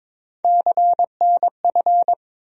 Morse code, heard as a keyed tone, about 22 wpm, 700 Hz. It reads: CNF
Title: USA Call Sign Postfixes - 3 Letters 22wpm